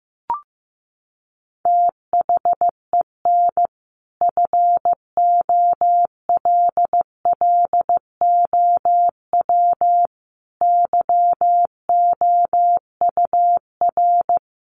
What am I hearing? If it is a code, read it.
Morse code, heard as a keyed tone, about 15 wpm, 700 Hz. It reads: THEN FOLLOW YOUR
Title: Sentences from Top 1400 Words 15wpm